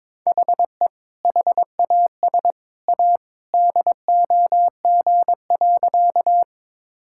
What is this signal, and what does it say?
Morse code, heard as a keyed tone, about 21 wpm, 700 Hz. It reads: HE HAS A DOG.
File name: practice / 22wpm / Sentences with 2-Word Phrases - 22wpm